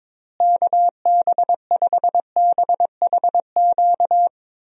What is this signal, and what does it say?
Morse code, heard as a keyed tone, about 22 wpm, 700 Hz. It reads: KB5BHQ
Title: USA Call Signs - 6 Characters - Repeated 3x - 22wpm